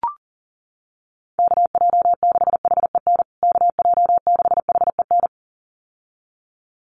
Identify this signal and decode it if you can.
Morse code, heard as a keyed tone, about 40 wpm, 700 Hz. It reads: KJ6HEN KJ6HEN
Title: USA Call Signs Repeated 2x 40wpm